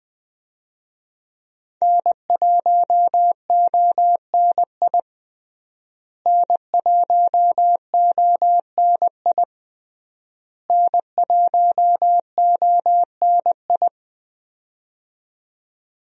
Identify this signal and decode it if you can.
Morse code, heard as a keyed tone, about 21 wpm, 700 Hz. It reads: N1ONI N1ONI N1ONI
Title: USA Call Signs - 5 Characters - Repeated 3x - 20wpm